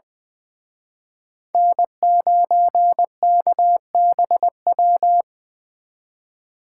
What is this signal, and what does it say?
Morse code, heard as a keyed tone, about 24 wpm, 700 Hz. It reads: N9KBW
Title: Speed Racing - USA 1x3 Call Signs 20wpm